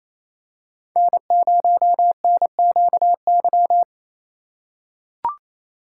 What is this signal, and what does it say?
Morse code, heard as a keyed tone, about 28 wpm, 700 Hz. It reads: N0NQY
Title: USA Call Signs - 4x Character Spacing - 28wpm